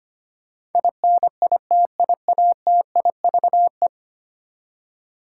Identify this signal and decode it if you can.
Morse code, heard as a keyed tone, about 25 wpm, 700 Hz. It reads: INITIATIVE
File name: Ten Letter Words - 25wpm